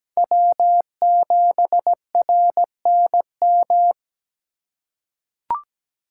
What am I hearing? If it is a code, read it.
Morse code, heard as a keyed tone, about 17 wpm, 700 Hz. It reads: W7RNM
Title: USA Call Signs - 5 Characters - 3x Character Spacing - 17wpm